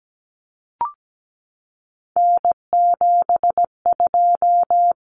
Morse code, heard as a keyed tone, about 17 wpm, 690 Hz. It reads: N72
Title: Common Cut Numbers - 3-Digit 17wpm